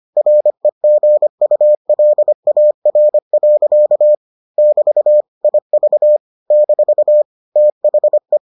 Morse code, heard as a keyed tone, about 25 wpm, 600 Hz. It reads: REGULAR. <BT> IV - THE